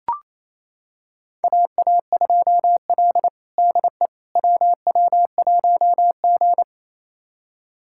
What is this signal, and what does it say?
Morse code, heard as a keyed tone, about 27 wpm, 700 Hz. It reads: AA2L DE WW1G
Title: USA Callsign DE Callsign - 4 Characters - 28wpm